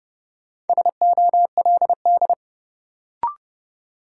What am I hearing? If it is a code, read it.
Morse code, heard as a keyed tone, about 30 wpm, 700 Hz. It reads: SOLD
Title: Sentences from Top 1400 Words - New Word Drill Alt - 30wpm